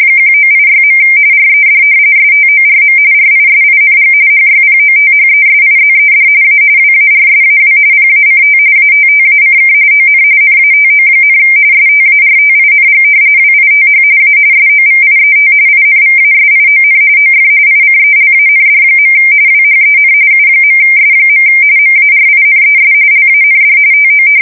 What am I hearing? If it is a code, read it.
RTTY, about 45.45 baud. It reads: 1986. RED STORM RISING HYPOTHESIZED A WARSAW PACT ATTACK ON NATO, STARTING WITH A SOVIET INVASION OF ICELAND. WHEN U.S. MARINES LAND TO
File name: fsk170 / battleshipnj